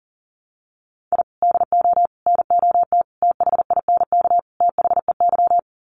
Morse code, heard as a keyed tone, about 40 wpm, 700 Hz. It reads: I DO NOT THINK THEY
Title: Sets of 5 Words 40wpm